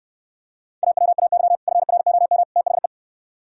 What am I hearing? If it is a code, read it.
Morse code, heard as a keyed tone, about 66 wpm, 700 Hz. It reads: AWAY FROM THE